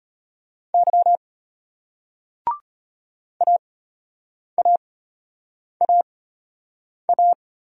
Morse code, heard as a keyed tone, about 33 wpm, spaced as for 8 wpm, 700 Hz. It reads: Y AAAA